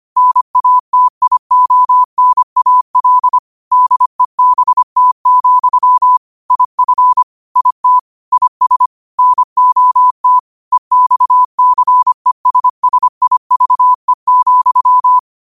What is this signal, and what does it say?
Morse code, heard as a keyed tone, about 25 wpm, 1000 Hz. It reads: NATIONAL DEBT, IF IT IS NOT EXCESSIVE,
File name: qotdfiles / 25wpm